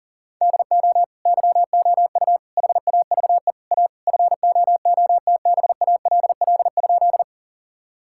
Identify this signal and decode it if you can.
Morse code, heard as a keyed tone, about 40 wpm, 700 Hz. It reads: DO YOU HAVE A FOOTBALL?
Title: Sentences from Top 1600 Words 40wpm